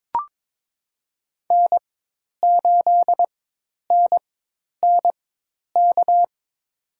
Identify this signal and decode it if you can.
Morse code, heard as a keyed tone, about 22 wpm, spaced as for 10 wpm, 700 Hz. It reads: N8NNK
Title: USA Call Signs - 5 Characters - 4x Character Spacing - 22wpm